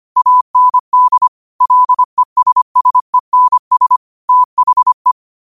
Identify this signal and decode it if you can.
Morse code, heard as a keyed tone, about 25 wpm, 1000 Hz. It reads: AND LESSENS THE
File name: qotdfiles / 25wpm